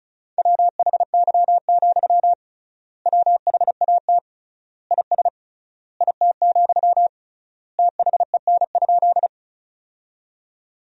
Morse code, heard as a keyed tone, about 36 wpm, 700 Hz. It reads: WHY, WHAT IS IT, THEN?
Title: 3x Word-Spacing - Sentences from Top 200 Words - 35wpm